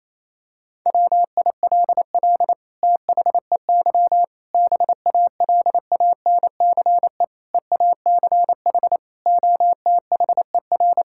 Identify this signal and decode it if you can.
Morse code, heard as a keyed tone, about 28 wpm, 700 Hz. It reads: WILL THEY BALANCE EACH OTHER
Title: Sentences from Top 1000 Words - Review Part 6 - 28wpm